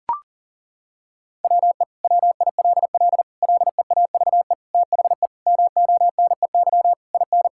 Morse code, heard as a keyed tone, about 40 wpm, 700 Hz. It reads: WE WILL LEAVE THE MONEY IN
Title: Sentences from Top 1000 Words - Review Part 6 - 40wpm